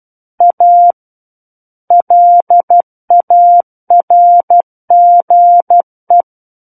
Morse code, heard as a keyed tone, about 12 wpm, 700 Hz. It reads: A LARGE
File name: Book One / 03-Chapter 3 - 12 WPM